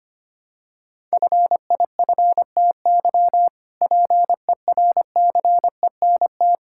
Morse code, heard as a keyed tone, about 25 wpm, 700 Hz. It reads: FIFTY PERCENT
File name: practice / 25wpm / Sentences with 3-Word Phrases - 25wpm